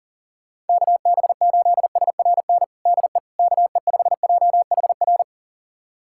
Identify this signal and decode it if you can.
Morse code, heard as a keyed tone, about 40 wpm, 700 Hz. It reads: KB8SRN DE KE5JHR
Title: USA Callsign DE Callsign - 6 Characters - 40wpm